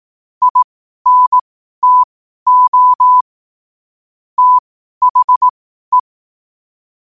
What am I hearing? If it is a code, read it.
Morse code, heard as a keyed tone, about 18 wpm, spaced as for 12 wpm, 1000 Hz. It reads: INTO THE
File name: qotdfiles / 12wpm